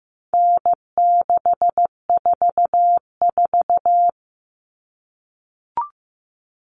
Morse code, heard as a keyed tone, about 15 wpm, 700 Hz. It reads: N644
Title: Common Cut Numbers - 4-Digit 15wpm